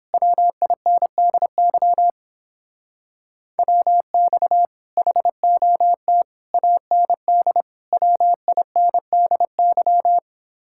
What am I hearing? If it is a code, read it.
Morse code, heard as a keyed tone, about 29 wpm, 700 Hz. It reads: WINDY WX HOT AND WINDY